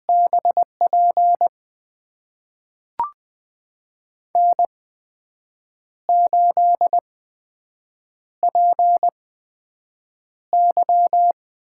Morse code, heard as a keyed tone, about 20 wpm, spaced as for 7 wpm, 700 Hz. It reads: BP N8PY
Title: USA Call Signs - 4 Characters - 8x Character Spacing - 20wpm